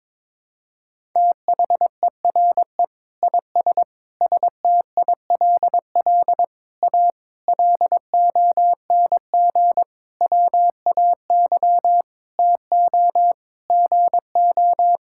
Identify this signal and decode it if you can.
Morse code, heard as a keyed tone, about 23 wpm, 700 Hz. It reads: THERE IS STILL A LONG WAY TO GO
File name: practice / 22wpm / Sets of 8 Words - 22wpm